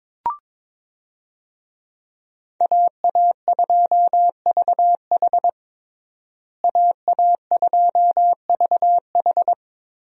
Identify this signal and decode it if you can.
Morse code, heard as a keyed tone, about 22 wpm, 700 Hz. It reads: AA2VH AA2VH
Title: USA Call Signs - 5 Characters - Repeated 3x - 22wpm